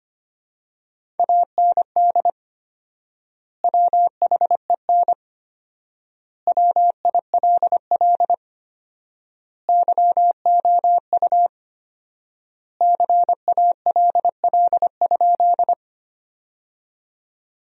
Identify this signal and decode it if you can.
Morse code, heard as a keyed tone, about 25 wpm, 700 Hz. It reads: AND WHEN WILL YOU CALL?
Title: 4x Word-Spacing - Sentences from Top 100 Words - 25wpm